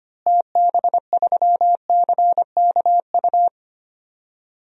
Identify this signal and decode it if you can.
Morse code, heard as a keyed tone, about 25 wpm, 700 Hz. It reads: TB3CKU
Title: DXCC-100 Call Signs - 6 Characters - Repeated 2x - 25wpm